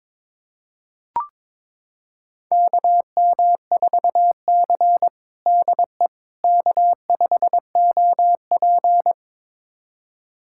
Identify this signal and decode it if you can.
Morse code, heard as a keyed tone, about 22 wpm, 700 Hz. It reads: KM4C DE K5OP
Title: USA Callsign DE Callsign - 4 Characters - 22wpm